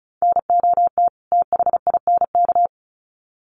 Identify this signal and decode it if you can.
Morse code, heard as a keyed tone, about 34 wpm, 700 Hz. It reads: NOT THINK
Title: Sets of 2 Words 35wpm